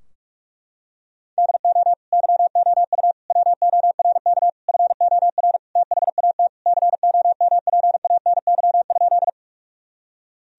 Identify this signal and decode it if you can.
Morse code, heard as a keyed tone, about 42 wpm, 700 Hz. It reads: DO YOU WORK FOR THAT COMPANY?
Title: Sentences from Top 1000 Words - Review Part 3 - 45wpm